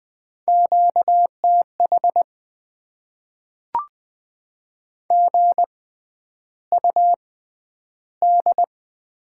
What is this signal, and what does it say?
Morse code, heard as a keyed tone, about 20 wpm, spaced as for 10 wpm, 700 Hz. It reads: QTH GUD
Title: Top 20 QSO Elements - 6x Character Spacing - 20wpm